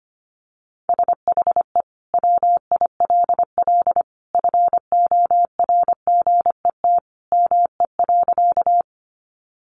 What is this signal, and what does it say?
Morse code, heard as a keyed tone, about 25 wpm, 700 Hz. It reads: SHE WILL FORGET ME.
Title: Sentences from Top 1200 Words 25wpm